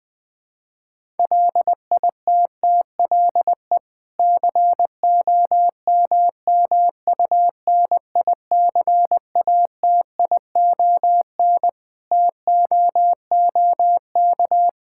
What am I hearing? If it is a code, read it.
Morse code, heard as a keyed tone, about 20 wpm, 700 Hz. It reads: LITTLE COMMUNICATION TOOK
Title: Sentences from Top 1000 Words - Review Part 6 - 20wpm